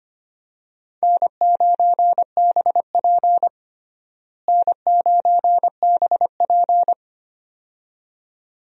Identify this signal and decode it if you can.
Morse code, heard as a keyed tone, about 25 wpm, 700 Hz. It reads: N9BP N9BP
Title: USA Call Signs - 4 Characters - Repeated 2x - 25wpm